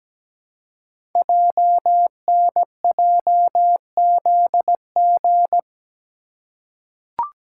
Morse code, heard as a keyed tone, about 17 wpm, 700 Hz. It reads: JNJZG
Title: Code Groups - 5 Letters - 17wpm